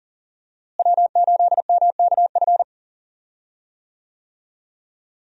Morse code, heard as a keyed tone, about 37 wpm, 700 Hz. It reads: W8MKF
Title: USA Call Signs - 5 Characters - Repeated 2x - 40wpm